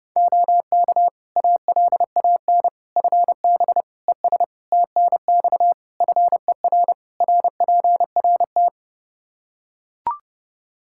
Morse code, heard as a keyed tone, about 30 wpm, 700 Hz. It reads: OK ALAN FB ES TNX FER RPRT
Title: Practice QSO - Exchange 2 - Info - 30wpm